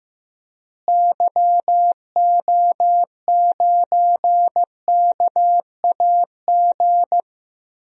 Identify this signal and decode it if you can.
Morse code, heard as a keyed tone, about 16 wpm, 700 Hz. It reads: YO9KAG
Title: DXCC-100 Call Signs - 6 Characters - Repeated 3x - 15wpm